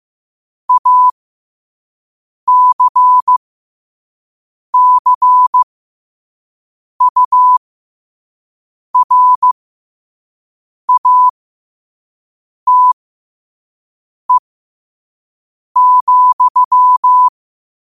Morse code, heard as a keyed tone, about 15 wpm, spaced as for 5 wpm, 1000 Hz. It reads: ACCURATE,
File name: qotdfiles / 5wpm